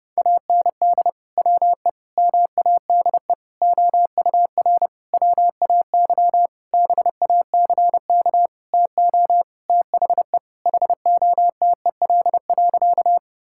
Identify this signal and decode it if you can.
Morse code, heard as a keyed tone, about 30 wpm, 700 Hz. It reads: AND WE MADE OUR WAY BACK TO THE HOTEL.
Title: Sentences from Top 1600 Words 30wpm